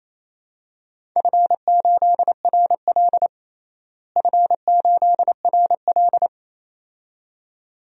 Morse code, heard as a keyed tone, about 28 wpm, 700 Hz. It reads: F8RL F8RL
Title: DXCC-100 Call Signs - 4 Characters - Repeated 2x - 28wpm